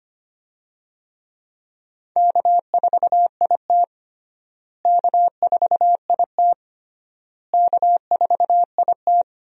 Morse code, heard as a keyed tone, about 25 wpm, 700 Hz. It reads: K4IT K4IT K4IT
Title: USA Call Signs - 4 Characters - Repeated 3x - 25wpm